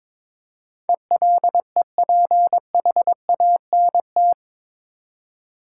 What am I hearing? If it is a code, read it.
Morse code, heard as a keyed tone, about 22 wpm, 700 Hz. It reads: ELEPHANT
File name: practice / 22wpm / Eight Letter Words - 22wpm